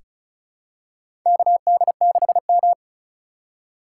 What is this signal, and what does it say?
Morse code, heard as a keyed tone, about 34 wpm, 700 Hz. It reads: KD6M
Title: USA Call Signs - 4 Characters - 3x Character Spacing - 35wpm